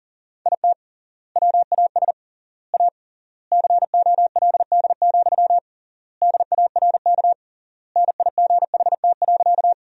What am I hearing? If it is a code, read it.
Morse code, heard as a keyed tone, about 40 wpm, 700 Hz. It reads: IT WAS A COLD, DARK NIGHT.
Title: 3x Word-Spacing - Sentences from Top 500 Words - 40wpm